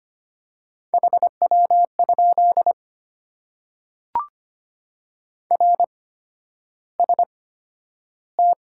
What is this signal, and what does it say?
Morse code, heard as a keyed tone, about 25 wpm, spaced as for 11 wpm, 690 Hz. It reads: HW? RST
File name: Top 90 QSO Elements - 8x Character Spacing - 25wpm